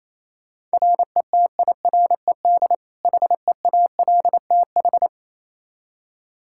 Morse code, heard as a keyed tone, about 28 wpm, 700 Hz. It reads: RETIRED HEALTH
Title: Practice QSO - Exchange 3 - More Info - 28wpm